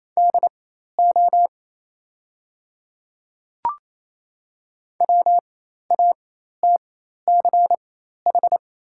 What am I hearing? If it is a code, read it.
Morse code, heard as a keyed tone, about 28 wpm, spaced as for 13 wpm, 700 Hz. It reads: DO WATCH
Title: 4x Character Spacing - Top 300 Words - 28wpm